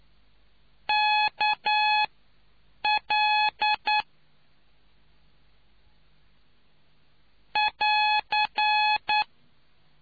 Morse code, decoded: KL <AR>

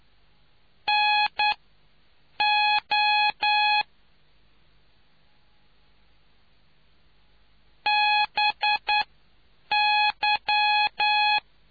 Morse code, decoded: NO BY

